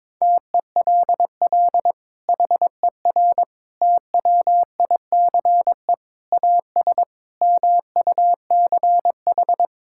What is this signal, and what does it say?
Morse code, heard as a keyed tone, about 22 wpm, 700 Hz. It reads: TELL HER TWICE AS MUCH
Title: Sentences from Top 1600 Words 22wpm